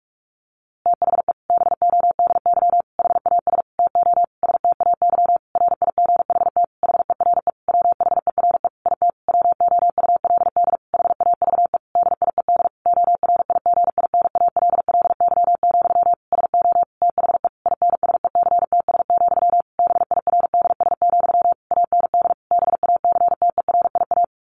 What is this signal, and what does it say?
Morse code, heard as a keyed tone, about 44 wpm, 700 Hz. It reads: THE BODY HAS TO STAY RIGHT HERE WHERE IT WOULD HAVE DIED ORIGINALLY, SO THE INSECTS, BIRDS, AND BACTERIA
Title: A Sound Of Thunder 45wpm